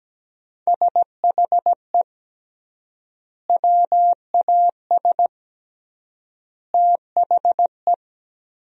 Morse code, heard as a keyed tone, about 17 wpm, 700 Hz. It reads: SHE WAS THE
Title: 3x Word-Spacing - Sentences from Top 200 Words - 17wpm